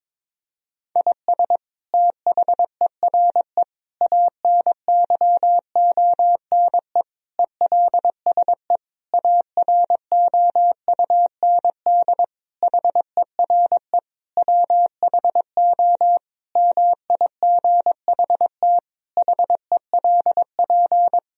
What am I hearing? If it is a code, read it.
Morse code, heard as a keyed tone, about 22 wpm, 700 Hz. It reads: IS THERE ANYONE ELSE AROUND HERE WHO MIGHT HELP
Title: Sentences from Top 1000 Words - Review Part 3 - 22wpm